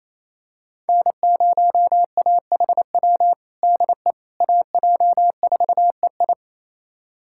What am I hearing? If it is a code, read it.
Morse code, heard as a keyed tone, about 28 wpm, 700 Hz. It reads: N0AHW DE AJ4EI
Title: USA Callsign DE Callsign - 5 Characters - 28wpm